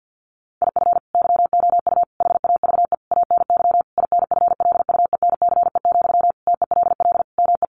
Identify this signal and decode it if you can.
Morse code, heard as a keyed tone, about 50 wpm, 700 Hz. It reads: IF YOU HAVE ANY INFLUENCE, TELL ME
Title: Sentences from Top 1100 Words 50wpm